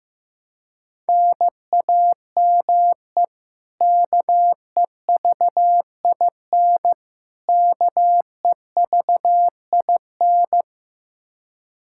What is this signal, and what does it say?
Morse code, heard as a keyed tone, about 16 wpm, 700 Hz. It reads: NAME KEVIN KEVIN
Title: Practice QSO - Exchange 1 - Beginning - 15wpm